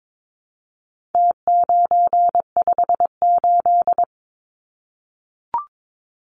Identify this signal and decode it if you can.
Morse code, heard as a keyed tone, about 22 wpm, 700 Hz. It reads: T958
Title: Numbers - Contest Style - 4-Digit 22wpm